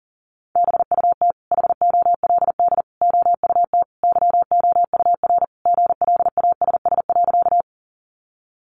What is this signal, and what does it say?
Morse code, heard as a keyed tone, about 40 wpm, 700 Hz. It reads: BUT HOLD OUT YOUR GLASS.
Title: Sentences from Top 1300 Words 40wpm